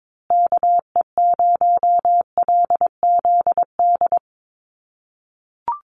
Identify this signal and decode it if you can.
Morse code, heard as a keyed tone, about 22 wpm, 700 Hz. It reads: KE0LZD
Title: USA Call Signs Repeated 2x 22wpm